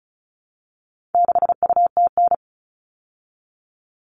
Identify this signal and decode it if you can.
Morse code, heard as a keyed tone, about 34 wpm, 700 Hz. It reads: 6UTN
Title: All Cut Numbers - 4-Digit 35wpm